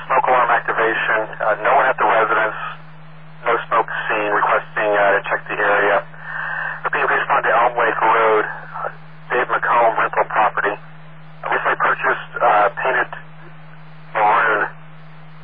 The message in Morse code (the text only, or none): none